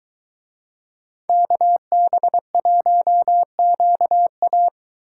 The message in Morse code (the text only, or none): KB1QA